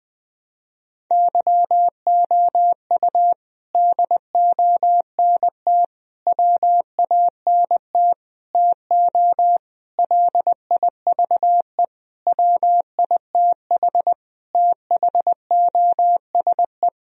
Morse code, heard as a keyed tone, about 21 wpm, 700 Hz. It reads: YOU DONT WANT TO LIVE WITH THOSE
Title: Sentences from Top 1000 Words - Review Part 1 - 20wpm